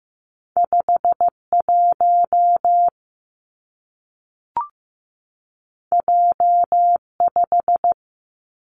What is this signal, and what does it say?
Morse code, heard as a keyed tone, about 15 wpm, 700 Hz. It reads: 51 J5
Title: Two Challenging Letter-Number Combinations 15wpm